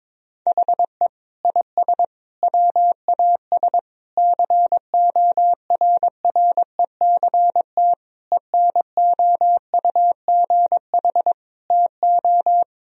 Morse code, heard as a keyed tone, about 22 wpm, 700 Hz. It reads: HE IS WAS CORRECT ENOUGH TO
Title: Sentences from Top 1000 Words - Review Part 2 - 22wpm